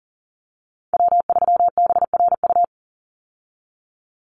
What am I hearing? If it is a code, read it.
Morse code, heard as a keyed tone, about 40 wpm, 700 Hz. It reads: W3BRU
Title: USA Call Signs 40wpm